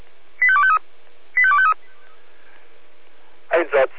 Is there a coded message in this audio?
Yes: five-tone selective call (ZVEI) from 0.4 to 1.7 s, heard as quick runs of five tones, tones of about 70 ms, with speech. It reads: 85253 85253